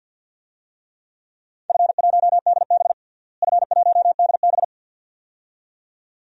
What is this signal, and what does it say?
Morse code, heard as a keyed tone, about 47 wpm, 700 Hz. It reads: F1DB F1DB
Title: DXCC-100 Call Signs - 4 Characters - Repeated 2x - 50wpm